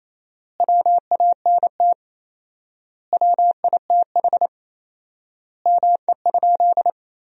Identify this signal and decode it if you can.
Morse code, heard as a keyed tone, about 28 wpm, 700 Hz. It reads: WANT WITH ME?
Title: 4x Word-Spacing - Sentences from Top 200 Words - 28wpm